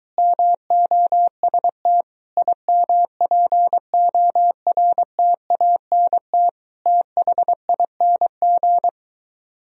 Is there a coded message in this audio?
Yes: Morse code from 0.2 to 8.9 s, heard as a keyed tone, about 23 wpm, 700 Hz. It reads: MOST IMPORTANT THING